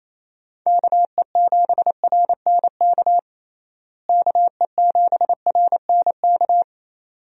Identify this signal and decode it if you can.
Morse code, heard as a keyed tone, about 28 wpm, 700 Hz. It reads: KE7RNK KE7RNK
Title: USA Call Signs - 6 Characters - Repeated 2x - 28wpm